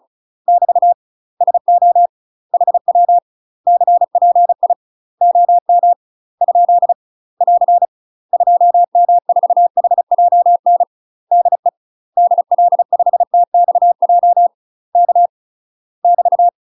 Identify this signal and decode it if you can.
Morse code, heard as a keyed tone, about 34 wpm, 700 Hz. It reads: <BT> SO HW CPI OM ? <AR> 2M4HJN DE DL5TXJ K <BT>
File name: UK MASTER QSOs TRAINING / 35WPM / UK MASTER QSOs 0034